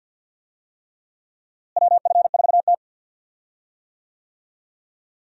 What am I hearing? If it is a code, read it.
Morse code, heard as a keyed tone, about 52 wpm, 700 Hz. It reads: WF4T
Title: USA Call Signs - 4 Characters - Repeated 3x - 50wpm